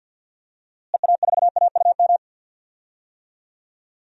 Morse code, heard as a keyed tone, about 51 wpm, 700 Hz. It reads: EA4AUM
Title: DXCC-100 Call Signs - 6 Characters - Repeated 3x - 50wpm